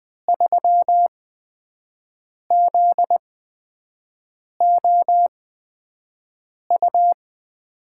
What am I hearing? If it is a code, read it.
Morse code, heard as a keyed tone, about 20 wpm, spaced as for 5 wpm, 700 Hz. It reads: 3ZOU
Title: USA Call Signs - 6 Characters - 8x Character Spacing - 20wpm